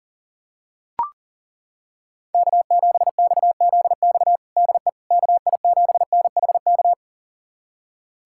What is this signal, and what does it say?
Morse code, heard as a keyed tone, about 37 wpm, 700 Hz. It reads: K7XZX DE KI7NHK
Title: USA Callsign DE Callsign 40wpm